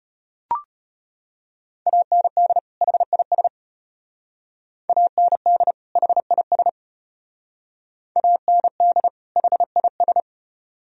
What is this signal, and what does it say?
Morse code, heard as a keyed tone, about 33 wpm, 700 Hz. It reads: AND HIS AND HIS AND HIS